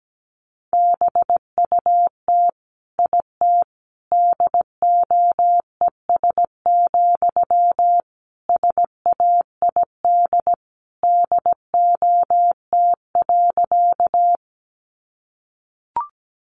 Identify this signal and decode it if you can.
Morse code, heard as a keyed tone, about 17 wpm, 700 Hz. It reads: BUT IT DOES, SAID DOT.
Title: Sentences from Top 1500 Words 17wpm